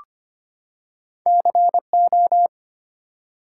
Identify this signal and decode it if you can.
Morse code, heard as a keyed tone, about 25 wpm, 700 Hz. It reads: CO
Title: US States and Canadian Provinces 25wpm